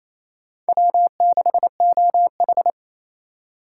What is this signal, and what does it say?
Morse code, heard as a keyed tone, about 28 wpm, 700 Hz. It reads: W6OH
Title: USA Call Signs - 4 Characters - 2x Character Spacing - 28wpm